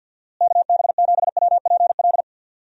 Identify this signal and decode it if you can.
Morse code, heard as a keyed tone, about 51 wpm, 700 Hz. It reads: KB7WPL
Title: USA Call Signs - 6 Characters - Repeated 2x - 50wpm